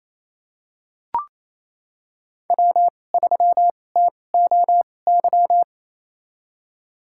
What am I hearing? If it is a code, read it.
Morse code, heard as a keyed tone, about 28 wpm, spaced as for 20 wpm, 700 Hz. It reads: W3TOY
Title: USA Call Signs - 5 Characters - 2x Character Spacing - 28wpm